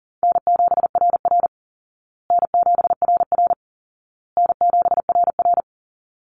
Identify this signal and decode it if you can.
Morse code, heard as a keyed tone, about 40 wpm, 700 Hz. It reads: N7RR N7RR N7RR
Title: Call Signs Repeated 2x - 40wpm